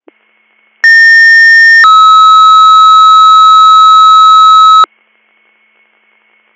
Two-tone page, heard as a one-second tone, then a three-second tone, about 1800 then 1290 Hz. It is heard from 0.8 to 4.9 s.